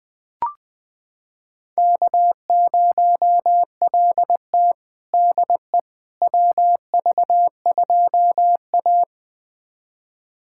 Morse code, heard as a keyed tone, about 20 wpm, 700 Hz. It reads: K0LT DE WV2A
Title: USA Callsign DE Callsign - 4 Characters - 20wpm